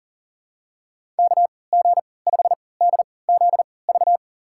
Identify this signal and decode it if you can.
Morse code, heard as a keyed tone, about 40 wpm, spaced as for 23 wpm, 700 Hz. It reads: KG5DZV
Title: USA Call Signs - 3x Character Spacing - 40wpm